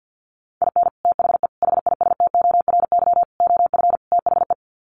Morse code, heard as a keyed tone, about 50 wpm, 700 Hz. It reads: IN THE HISTORY OF THE
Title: Sets of 5 Words 50wpm